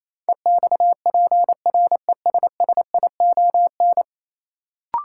Morse code, heard as a keyed tone, about 28 wpm, 700 Hz. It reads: EXPRESSION
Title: Sentences from Top 1200 Words - New Word Drill - 28wpm